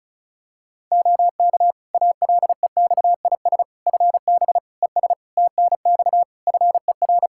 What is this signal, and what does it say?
Morse code, heard as a keyed tone, about 34 wpm, 700 Hz. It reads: OK ALEXIS FB ES TNX FER
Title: Practice QSO - Exchange 2 - Info - 35wpm